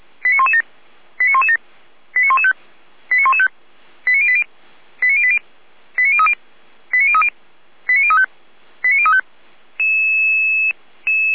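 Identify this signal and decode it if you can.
Five-tone selective call (ZVEI), heard as quick runs of five tones, tones of about 70 ms. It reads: 89117 89117 89115 89115 89080 89080 89030 89030 89035 89035
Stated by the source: Five-tone ZVEI selective call received by SDR, from the Orlamünde fire brigade (feuerwehr-orlamuende) audio archive.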